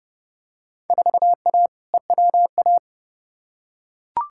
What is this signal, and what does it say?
Morse code, heard as a keyed tone, about 30 wpm, 700 Hz. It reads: 4A EWA
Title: Field Day Exchanges - 30wpm